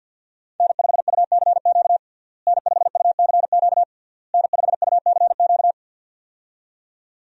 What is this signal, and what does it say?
Morse code, heard as a keyed tone, about 52 wpm, 700 Hz. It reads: N5UCQ N5UCQ N5UCQ
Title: USA Call Signs - 5 Characters - Repeated 3x - 50wpm